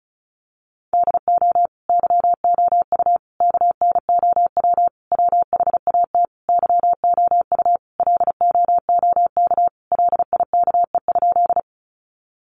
Morse code, heard as a keyed tone, about 34 wpm, 700 Hz. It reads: DO YOU KNOW WHAT YOU LOOK LIKE?
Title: Sentences from Top 100 Words 35wpm